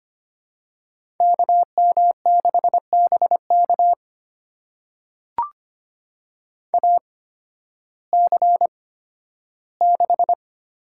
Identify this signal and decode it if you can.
Morse code, heard as a keyed tone, about 25 wpm, spaced as for 13 wpm, 700 Hz. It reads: KM6BK AC6
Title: USA Call Signs - 5 Characters - 8x Character Spacing - 25wpm